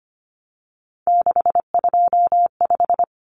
Morse code, heard as a keyed tone, about 25 wpm, 700 Hz. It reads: 625
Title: Common Cut Numbers - 3-Digit 25wpm